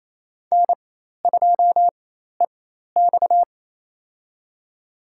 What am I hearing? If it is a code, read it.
Morse code, heard as a keyed tone, about 28 wpm, spaced as for 13 wpm, 700 Hz. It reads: N2EX
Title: USA Call Signs - 4x Character Spacing - 28wpm